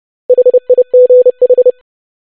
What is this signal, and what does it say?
Morse code, heard as a keyed tone, about 30 wpm, 500 Hz. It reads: HIGH
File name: basic500-25wpm